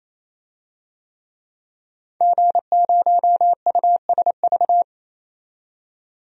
Morse code, heard as a keyed tone, about 28 wpm, 700 Hz. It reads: G0USV
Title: DXCC-100 Call Signs - 5 Characters - 28wpm